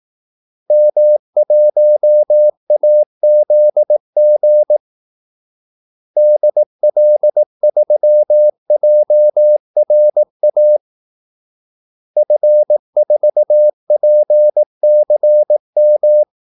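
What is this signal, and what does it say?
Morse code, heard as a keyed tone, about 18 wpm, 600 Hz. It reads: M1AZG DL3JRA F4PCM
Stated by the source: Morse call-sign practice at 18 wpm.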